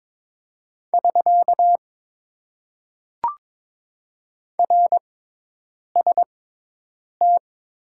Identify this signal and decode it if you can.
Morse code, heard as a keyed tone, about 22 wpm, spaced as for 8 wpm, 700 Hz. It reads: <SK> RST
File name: Top 90 QSO Elements - 6x Character Spacing - 22wpm